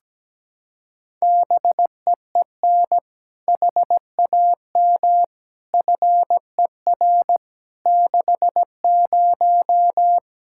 Morse code, heard as a keyed tone, about 17 wpm, 700 Hz. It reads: BEEN HAM FER 60
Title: Practice QSO - Exchange 3 - More Info - 17wpm